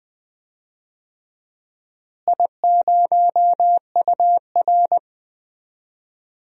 Morse code, heard as a keyed tone, about 20 wpm, 700 Hz. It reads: I0UR
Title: DXCC-100 Call Signs - 4 Characters - Repeated 2x - 20wpm